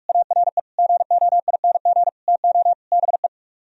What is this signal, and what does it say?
Morse code, heard as a keyed tone, about 50 wpm, 700 Hz. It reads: ARE GOING TO BE